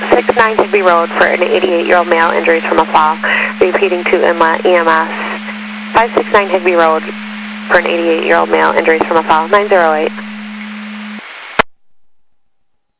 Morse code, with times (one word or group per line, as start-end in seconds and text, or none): none